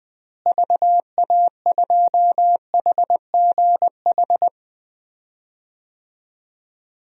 Morse code, heard as a keyed tone, about 21 wpm, 700 Hz. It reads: VA2HGH
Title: DXCC-100 Call Signs - 6 Characters - Repeated 2x - 20wpm